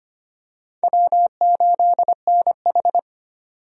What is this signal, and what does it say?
Morse code, heard as a keyed tone, about 25 wpm, 700 Hz. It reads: W8NH
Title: USA Call Signs - 4 Characters - 25wpm